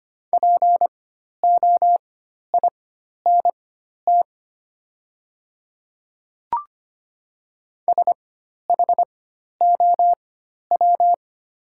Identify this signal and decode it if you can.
Morse code, heard as a keyed tone, about 25 wpm, spaced as for 12 wpm, 700 Hz. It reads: POINT SHOW
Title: 4x Character Spacing - Top 300 Words - 25wpm